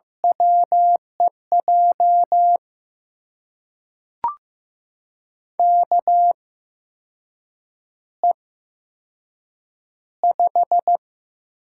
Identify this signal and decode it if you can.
Morse code, heard as a keyed tone, about 15 wpm, spaced as for 6 wpm, 700 Hz. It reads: WEJ KE5
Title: USA Call Signs - 6 Characters - 8x Character Spacing - 15wpm